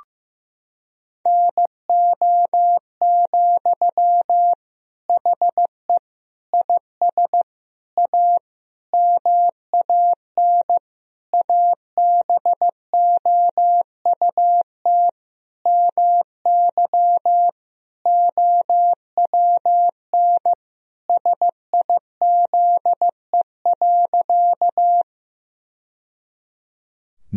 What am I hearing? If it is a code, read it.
Morse code, heard as a keyed tone, about 16 wpm, 700 Hz. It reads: NO, HE IS A MAN ABOUT MY OWN SIZE.
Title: Sentences from Top 1000 Words - Review Part 2 - 15wpm